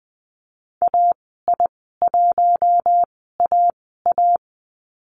Morse code, heard as a keyed tone, about 20 wpm, spaced as for 15 wpm, 700 Hz. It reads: AI1AA
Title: 042 Lesson 8g - Number 1 - Call Signs 20wpm